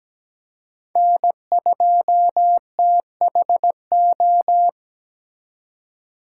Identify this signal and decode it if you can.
Morse code, heard as a keyed tone, about 20 wpm, 700 Hz. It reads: N2THO
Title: Speed Racing - USA Call Signs 17wpm